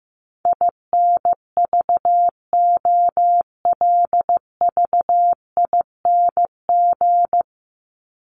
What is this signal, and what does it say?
Morse code, heard as a keyed tone, about 14 wpm, 700 Hz. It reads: INVOLVING
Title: ING Suffix - Top 100 Words with Base Words 15wpm